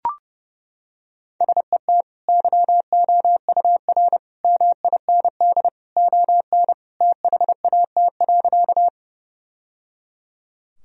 Morse code, heard as a keyed tone, about 30 wpm, 700 Hz. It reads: SET YOUR MIND ON THAT.
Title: Sentences from Top 1000 Words - Review Part 1 - 30wpm